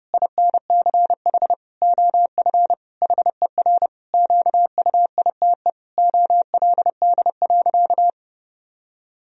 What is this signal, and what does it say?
Morse code, heard as a keyed tone, about 30 wpm, 700 Hz. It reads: INCH OF HER QUITE OLD.
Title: Sentences from Top 1000 Words - Review Part 3 - 30wpm